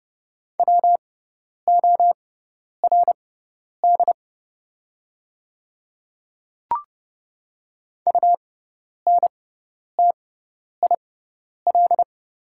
Morse code, heard as a keyed tone, about 30 wpm, spaced as for 10 wpm, 700 Hz. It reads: WORD UNTIL
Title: 6x Character Spacing - Top 300 Words - 30wpm